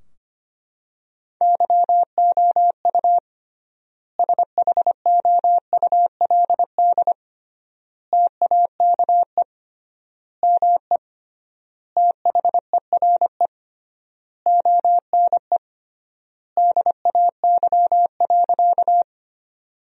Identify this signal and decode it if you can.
Morse code, heard as a keyed tone, about 25 wpm, 700 Hz. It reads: YOU SHOULD TAKE ME THERE ONE DAY.
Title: 3x Word-Spacing - Sentences from Top 300 Words - 25wpm